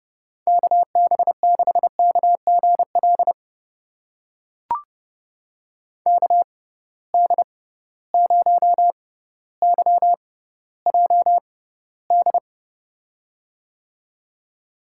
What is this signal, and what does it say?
Morse code, heard as a keyed tone, about 31 wpm, spaced as for 16 wpm, 700 Hz. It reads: KB6KGL KD0YJD
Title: USA Call Signs - 6 Characters - 6x Character Spacing - 30wpm